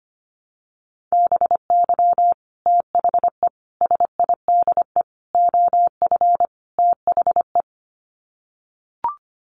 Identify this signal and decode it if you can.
Morse code, heard as a keyed tone, about 25 wpm, 700 Hz. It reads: BY THE SIDE OF THE
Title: Sets of 5 Words Encore2 25wpm